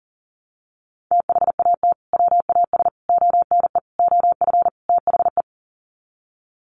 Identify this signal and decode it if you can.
Morse code, heard as a keyed tone, about 40 wpm, 700 Hz. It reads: THAT WAS ONE OF THE
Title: Sets of 5 Words 40wpm